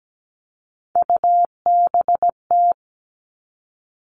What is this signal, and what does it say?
Morse code, heard as a keyed tone, about 17 wpm, 700 Hz. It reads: UBT
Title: All Cut Numbers - 3-Digit 17wpm